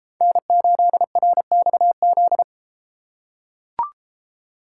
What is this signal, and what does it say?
Morse code, heard as a keyed tone, about 32 wpm, 700 Hz. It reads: N8RXZ